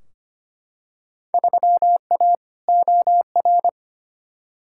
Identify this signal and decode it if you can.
Morse code, heard as a keyed tone, about 25 wpm, 700 Hz. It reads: 3A OR
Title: Field Day Exchanges - 25wpm